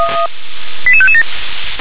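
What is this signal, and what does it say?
Five-tone selective call (ZVEI), heard as quick runs of five tones, tones of about 70 ms. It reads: 88557
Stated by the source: Five-tone ZVEI selective call 88783 received by SDR.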